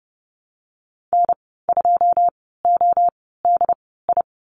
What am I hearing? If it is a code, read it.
Morse code, heard as a keyed tone, about 30 wpm, spaced as for 17 wpm, 700 Hz. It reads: N2ODI
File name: Lesson 14g - Number 2 - Call Signs 30wpm